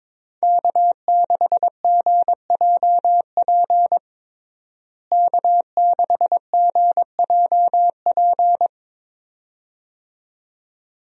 Morse code, heard as a keyed tone, about 21 wpm, 700 Hz. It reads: K6GJP K6GJP
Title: USA Call Signs - 5 Characters - Repeated 2x - 22wpm